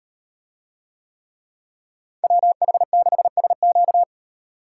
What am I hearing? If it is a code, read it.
Morse code, heard as a keyed tone, about 35 wpm, 700 Hz. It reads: WH6SQ